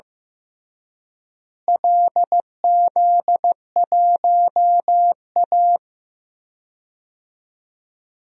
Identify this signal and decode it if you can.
Morse code, heard as a keyed tone, about 15 wpm, 700 Hz. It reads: LZ1A
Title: DXCC-100 Call Signs - 4 Characters - Repeated 3x - 15wpm